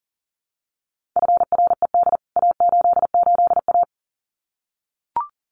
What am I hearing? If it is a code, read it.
Morse code, heard as a keyed tone, about 40 wpm, 700 Hz. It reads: FRED A88A
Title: CWT Exchange 40wpm